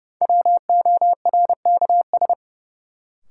Morse code, heard as a keyed tone, about 30 wpm, 700 Hz. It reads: WORKS
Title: Sentences from Top 1000 Words - Review Part 5 - New Word Drill - 30wpm